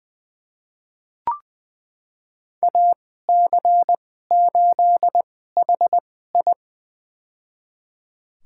Morse code, heard as a keyed tone, about 20 wpm, spaced as for 14 wpm, 700 Hz. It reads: AC8HI
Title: USA Call Signs - 2x Character Spacing - 20wpm